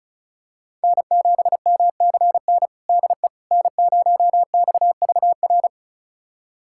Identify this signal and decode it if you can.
Morse code, heard as a keyed tone, about 36 wpm, 700 Hz. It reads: N7MCN DE N0XVR
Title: USA Callsign DE Callsign - 5 Characters - 35wpm